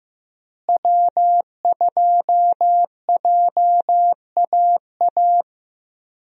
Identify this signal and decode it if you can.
Morse code, heard as a keyed tone, about 15 wpm, 700 Hz. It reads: W2JAA
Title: USA Call Signs - 5 Characters - Repeated 2x - 15wpm